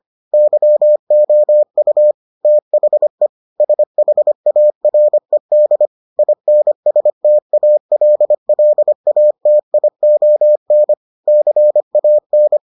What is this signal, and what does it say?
Morse code, heard as a keyed tone, about 25 wpm, 600 Hz. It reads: YOU THE SHARED INSTALLATION CAN